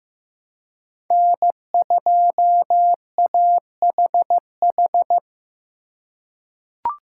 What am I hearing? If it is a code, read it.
Morse code, heard as a keyed tone, about 16 wpm, 700 Hz. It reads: N2AHH
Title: USA Call Signs - 5 Characters - 3x Character Spacing - 15wpm